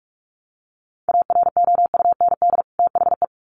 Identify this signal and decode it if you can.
Morse code, heard as a keyed tone, about 44 wpm, 700 Hz. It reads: AROUND THE
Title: Sets of 2 Words 45wpm